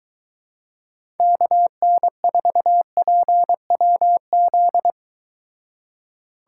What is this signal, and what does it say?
Morse code, heard as a keyed tone, about 23 wpm, 700 Hz. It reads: KN4PWZ